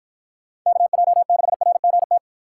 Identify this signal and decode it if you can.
Morse code, heard as a keyed tone, about 52 wpm, 700 Hz. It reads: KJ6RGT